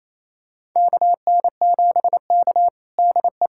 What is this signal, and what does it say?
Morse code, heard as a keyed tone, about 28 wpm, 700 Hz. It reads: KN7K DE
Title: USA Callsign DE Callsign - 4 Characters - 28wpm